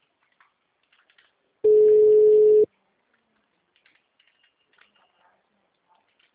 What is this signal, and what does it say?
Ringback tone (Europe, 425 Hz).